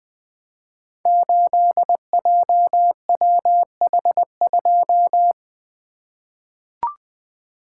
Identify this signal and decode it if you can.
Morse code, heard as a keyed tone, about 20 wpm, 700 Hz. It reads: 8JWH2
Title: Code Groups - 5 Letter:Numbers - 20wpm